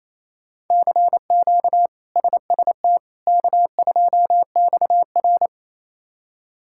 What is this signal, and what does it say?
Morse code, heard as a keyed tone, about 28 wpm, 700 Hz. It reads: CQ SST K2XR
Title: K1USN SST QSO Practice - 28wpm